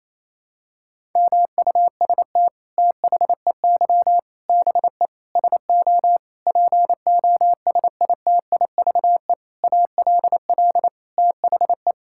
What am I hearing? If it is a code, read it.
Morse code, heard as a keyed tone, about 28 wpm, 700 Hz. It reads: MUST THEY BE SO POSITIVE ALL THE
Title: Sentences from Top 1000 Words - Review Part 5 - 28wpm